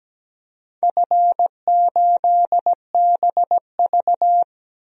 Morse code, heard as a keyed tone, about 17 wpm, 700 Hz. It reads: F8BV
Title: DXCC-100 Call Signs - 4 Characters - 17wpm